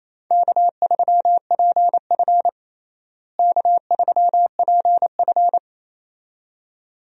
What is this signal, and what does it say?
Morse code, heard as a keyed tone, about 28 wpm, 700 Hz. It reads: K3PF K3PF
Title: USA Call Signs - 4 Characters - Repeated 2x - 28wpm